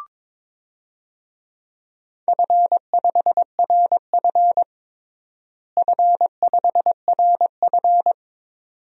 Morse code, heard as a keyed tone, about 22 wpm, 700 Hz. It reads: F5RF F5RF
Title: DXCC-100 Call Signs - 4 Characters - Repeated 2x - 22wpm